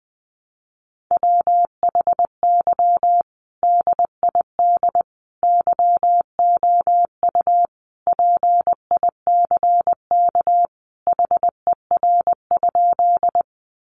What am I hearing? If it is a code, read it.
Morse code, heard as a keyed tone, about 19 wpm, 700 Hz. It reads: WHY DID YOU PICK HER?
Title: Sentences from Top 1000 Words 20wpm